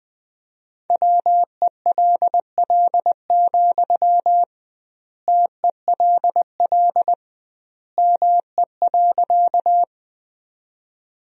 Morse code, heard as a keyed tone, about 20 wpm, 700 Hz. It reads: WELL, TELL ME.
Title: 2x Word-Spacing - Sentences from Top 200 Words - 20wpm